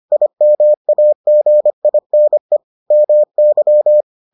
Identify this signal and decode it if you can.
Morse code, heard as a keyed tone, about 25 wpm, 600 Hz. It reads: IMAGINE MY